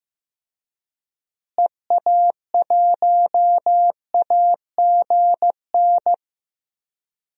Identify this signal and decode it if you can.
Morse code, heard as a keyed tone, about 15 wpm, 700 Hz. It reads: EA1AGN
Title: DXCC-100 Call Signs - 6 Characters - Repeated 3x - 15wpm